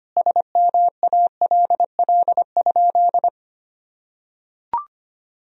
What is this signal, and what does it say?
Morse code, heard as a keyed tone, about 25 wpm, 700 Hz. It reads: SMALL?
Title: Sentences from Top 1000 Words - Review Part 1 - 25wpm